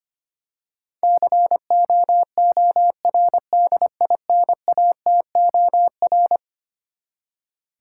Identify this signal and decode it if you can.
Morse code, heard as a keyed tone, about 25 wpm, 700 Hz. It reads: COORDINATOR
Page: morse-code-ninja-practice-25wpm